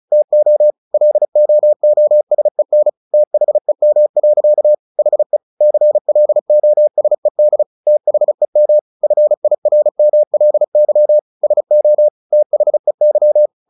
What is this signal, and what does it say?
Morse code, heard as a keyed tone, about 34 wpm, 600 Hz. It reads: TO LOOSEN THEM. HE CLOSED THEM FIRMLY SO THEY
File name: the old man and the sea 2 35wpm